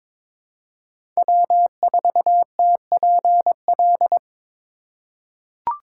Morse code, heard as a keyed tone, about 22 wpm, 700 Hz. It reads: W4TPL
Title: USA Call Signs - 5 Characters - 4x Character Spacing - 22wpm